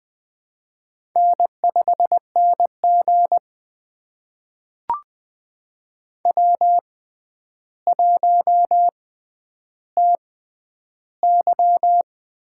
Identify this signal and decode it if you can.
Morse code, heard as a keyed tone, about 21 wpm, spaced as for 10 wpm, 700 Hz. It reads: N5NG W1TY
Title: USA Call Signs - 4 Characters - 6x Character Spacing - 20wpm